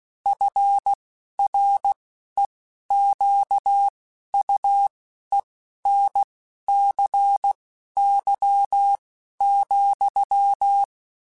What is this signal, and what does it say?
Morse code, heard as a keyed tone, about 16 wpm, spaced as for 11 wpm, 800 Hz. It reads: FREQUENCY,
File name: Technical Subjects / Technical Subjects-mp3s / XL @10